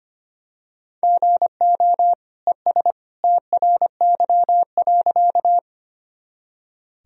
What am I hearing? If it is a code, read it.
Morse code, heard as a keyed tone, about 25 wpm, 700 Hz. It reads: GO ES TRY.
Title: Sentences from Top 200 Words and Abbreviations 25wpm